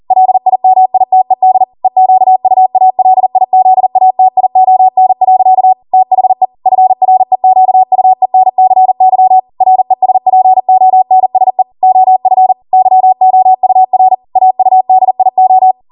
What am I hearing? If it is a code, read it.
Morse code, heard as a keyed tone, about 40 wpm, 750 Hz. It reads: LIMITED EQUALIZATION. THE FREQUENCY RESPONSE OF YOUR AUDIO